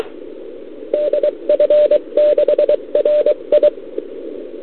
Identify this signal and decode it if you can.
Morse code, heard as a keyed tone, about 23 wpm, 590 Hz, with radio noise. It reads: DF6RI
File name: CQWWCW2021 / Doderna OL3Z